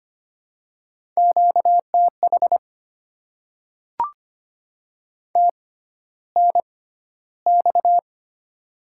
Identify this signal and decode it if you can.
Morse code, heard as a keyed tone, about 25 wpm, spaced as for 13 wpm, 700 Hz. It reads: QTH TNX
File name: Top 20 QSO Elements - 6x Character Spacing - 25wpm